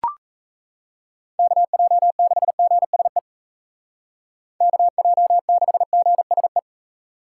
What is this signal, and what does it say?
Morse code, heard as a keyed tone, about 34 wpm, 700 Hz. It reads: KJ6GSE KJ6GSE
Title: Speed Racing - USA 2x3 Call Signs 28wpm